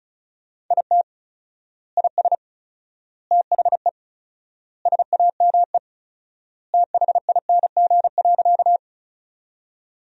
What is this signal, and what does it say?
Morse code, heard as a keyed tone, about 36 wpm, 700 Hz. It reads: IT IS THE SAME THING.
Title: 4x Word-Spacing - Sentences from Top 200 Words - 35wpm